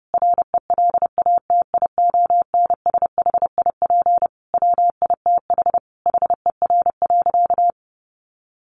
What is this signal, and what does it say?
Morse code, heard as a keyed tone, about 30 wpm, 700 Hz. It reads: RELATIONSHIP WITH HER.
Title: Sentences from Top 1100 Words 30wpm